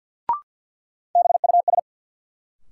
Morse code, heard as a keyed tone, about 52 wpm, 700 Hz. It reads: BUS
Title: Sentences from Top 1000 Words - Review Part 6 - New Word Drill - 50wpm